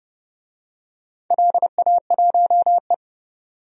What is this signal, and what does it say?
Morse code, heard as a keyed tone, about 32 wpm, 700 Hz. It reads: LA1E
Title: DXCC-100 Call Signs - 4 Characters - 30wpm